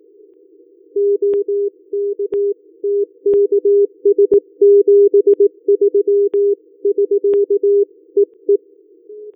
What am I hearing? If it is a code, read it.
Morse code, heard as a keyed tone, about 18 wpm, 400 Hz. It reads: OK TKS 73 <SK> EE
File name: qso001